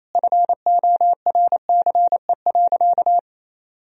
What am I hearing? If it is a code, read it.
Morse code, heard as a keyed tone, about 28 wpm, 700 Hz. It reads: FORCE.
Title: Sentences from Top 1000 Words - Review Part 4 - 28wpm